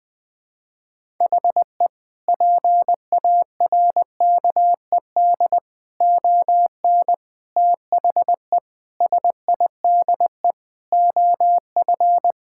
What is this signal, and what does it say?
Morse code, heard as a keyed tone, about 20 wpm, 700 Hz. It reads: HE PARKED ON THE SIDE OF
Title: Sentences with 4-Word Phrases - 20wpm